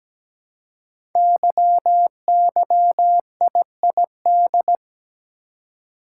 Morse code, heard as a keyed tone, about 17 wpm, 700 Hz. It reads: YYIID
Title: Code Groups - 5 Letter:Numbers - 17wpm